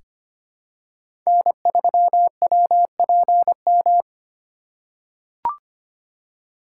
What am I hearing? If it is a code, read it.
Morse code, heard as a keyed tone, about 25 wpm, 700 Hz. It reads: N3WPM
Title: USA Call Signs - 5 Characters - 2x Character Spacing - 25wpm